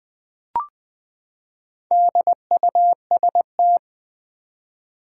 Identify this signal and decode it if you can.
Morse code, heard as a keyed tone, about 20 wpm, 700 Hz. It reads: DUST
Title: Four Letter Words - 20wpm